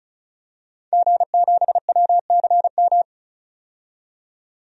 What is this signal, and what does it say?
Morse code, heard as a keyed tone, about 36 wpm, 700 Hz. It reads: G7WCM
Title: DXCC-100 Call Signs - 5 Characters - 35wpm